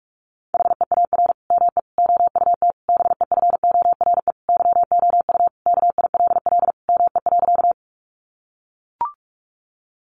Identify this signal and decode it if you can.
Morse code, heard as a keyed tone, about 44 wpm, 700 Hz. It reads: HEAR ME OUT BEFORE YOU KILL ME.
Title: Sentences from Top 1400 Words 45wpm